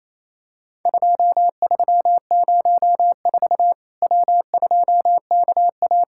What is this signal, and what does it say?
Morse code, heard as a keyed tone, about 28 wpm, 700 Hz. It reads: 2304 W2KA
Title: Worked All Europe DX Contest - QTC Exchange - 28wpm